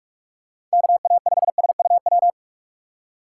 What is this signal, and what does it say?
Morse code, heard as a keyed tone, about 46 wpm, 700 Hz. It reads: KA5SUW
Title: USA Call Signs - 6x Character Spacing - 45wpm